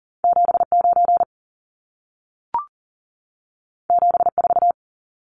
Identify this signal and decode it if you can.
Morse code, heard as a keyed tone, about 40 wpm, 700 Hz. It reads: 79 74
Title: Numbers - 2-Digit 40wpm